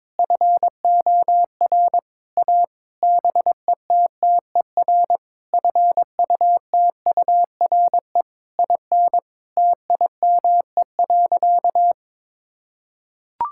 Morse code, heard as a keyed tone, about 22 wpm, 700 Hz. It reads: FOR A BETTER FUTURE IN TIME.
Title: Sentences from Top 1000 Words - Review Part 6 - 22wpm